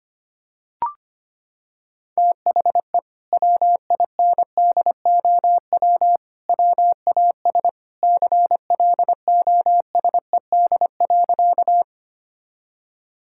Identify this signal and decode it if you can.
Morse code, heard as a keyed tone, about 25 wpm, 700 Hz. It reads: THE WINDOW WAS CLOSED.
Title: Sentences from Top 1600 Words 25wpm